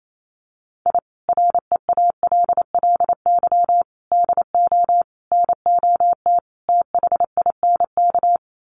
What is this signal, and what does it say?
Morse code, heard as a keyed tone, about 28 wpm, 700 Hz. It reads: I REALLY DO NOT THINK
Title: Sets of 5 Words Encore2 28wpm